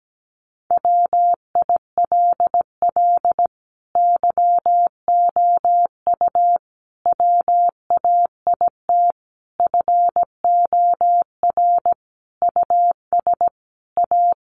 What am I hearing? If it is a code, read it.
Morse code, heard as a keyed tone, about 17 wpm, 700 Hz. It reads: WILL YOU WAIT FOR US A
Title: Sentences from Top 900 Words 17wpm